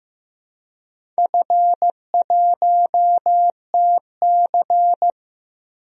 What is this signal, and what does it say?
Morse code, heard as a keyed tone, about 15 wpm, 700 Hz. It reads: F1TC
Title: DXCC-100 Call Signs - 4 Characters - Repeated 2x - 15wpm